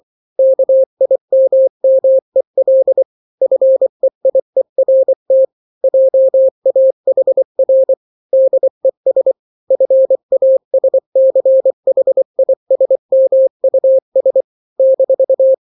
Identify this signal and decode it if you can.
Morse code, heard as a keyed tone, about 24 wpm, 530 Hz. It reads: KIMMEL FEIERT JAHR DES FASCHISMUS -
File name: daily / 23wpm